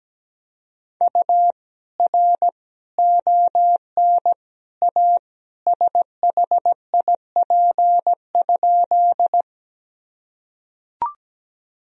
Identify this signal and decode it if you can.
Morse code, heard as a keyed tone, about 17 wpm, 700 Hz. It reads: U R ON A SHIP?
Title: Sentences from Top 400 Words and Abbreviations 17wpm